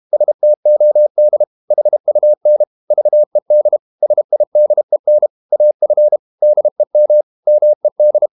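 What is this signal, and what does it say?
Morse code, heard as a keyed tone, about 32 wpm, 600 Hz. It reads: STOD HUN VED SIDEN AF DEM MED
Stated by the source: High-speed QRQ CW at 32 WPM.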